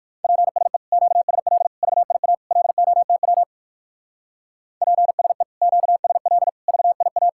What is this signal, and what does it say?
Morse code, heard as a keyed tone, about 64 wpm, 700 Hz. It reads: PSE QSL VIA LOTW PSE QSL VIA